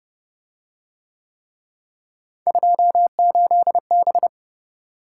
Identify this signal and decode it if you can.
Morse code, heard as a keyed tone, about 29 wpm, 700 Hz. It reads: 28B